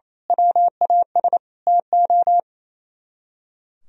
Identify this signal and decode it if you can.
Morse code, heard as a keyed tone, about 33 wpm, 700 Hz. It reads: WAS TO
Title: Speed-Racing - Sets of 5 Words 28wpm